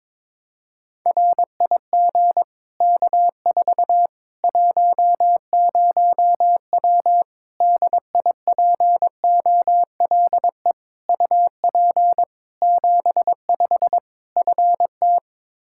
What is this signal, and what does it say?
Morse code, heard as a keyed tone, about 22 wpm, 700 Hz. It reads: RIG K4 10W DIPOLE UP 75 FT
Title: Practice QSO - Exchange 2 - Info - 22wpm